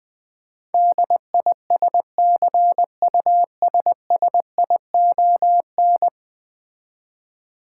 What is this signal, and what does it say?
Morse code, heard as a keyed tone, about 20 wpm, 700 Hz. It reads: DISCUSSION
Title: 10 Letter Words - First 3 Letters Given - 20wpm